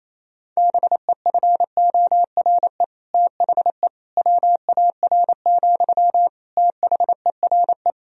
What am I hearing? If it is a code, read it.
Morse code, heard as a keyed tone, about 28 wpm, 700 Hz. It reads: BEFORE THE WAR, THERE
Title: Sentences from Top 1000 Words - Review Part 2 - 28wpm